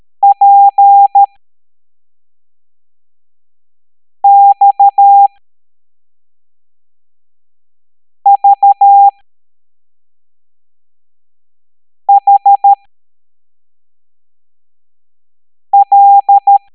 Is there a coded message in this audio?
Yes: Morse code from 0.2 to 16.6 s, heard as a keyed tone, about 13 wpm, spaced as for 3 wpm, 800 Hz. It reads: PXVHL